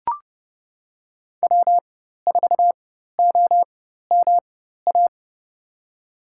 Morse code, heard as a keyed tone, about 30 wpm, spaced as for 14 wpm, 700 Hz. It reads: W4OMA
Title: USA Call Signs - 5 Characters - 4x Character Spacing - 30wpm